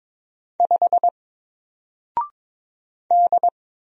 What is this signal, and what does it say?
Morse code, heard as a keyed tone, about 22 wpm, 700 Hz. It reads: 5 D
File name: Single-Letter-Number-22wpm